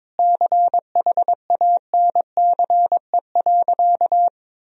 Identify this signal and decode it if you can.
Morse code, heard as a keyed tone, about 22 wpm, 700 Hz. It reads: CHANCE.
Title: Sentences from Top 1000 Words - Review Part 2 - 22wpm